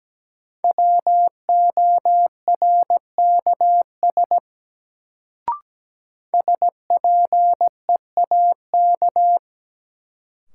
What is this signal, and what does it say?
Morse code, heard as a keyed tone, about 17 wpm, 700 Hz. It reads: WORKS SPEAK
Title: Sentences from Top 1100 Words - New Word Drill Alt - 17wpm